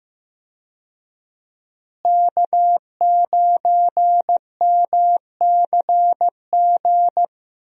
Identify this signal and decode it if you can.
Morse code, heard as a keyed tone, about 15 wpm, 700 Hz. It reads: K9MCG
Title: USA Call Signs - 5 Characters - Repeated 2x - 15wpm